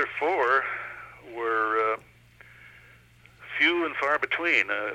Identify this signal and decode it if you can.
Emergency Alert System (SAME) tones, heard as harsh buzzing bursts, with speech. It is heard from 0.6 to 4.7 s.